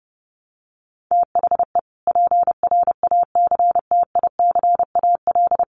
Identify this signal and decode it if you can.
Morse code, heard as a keyed tone, about 30 wpm, 700 Hz. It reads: THE PRACTICAL
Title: Sentences from Top 1400 Words 30wpm